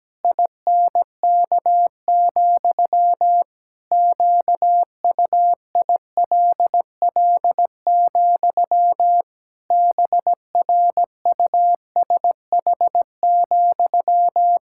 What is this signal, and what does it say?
Morse code, heard as a keyed tone, about 17 wpm, 700 Hz. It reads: INK, QUILL, BRUSH,
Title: Pangram Challenge v2 - 17wpm